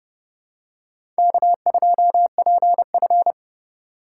Morse code, heard as a keyed tone, about 30 wpm, 700 Hz. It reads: K2PF
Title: USA Call Signs - 4 Characters - 2x Character Spacing - 30wpm